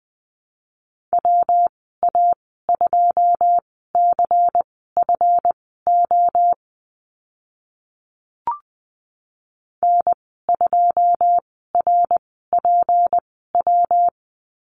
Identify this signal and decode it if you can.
Morse code, heard as a keyed tone, about 20 wpm, spaced as for 14 wpm, 700 Hz. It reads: WA2CFO N2RPW
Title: 161 Lesson 26g - Letter g - Call Signs 20wpm